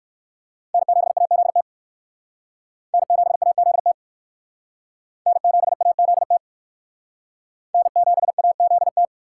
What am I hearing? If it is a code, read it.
Morse code, heard as a keyed tone, about 62 wpm, 700 Hz. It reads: N7AZT N7AZT N7AZT N7AZT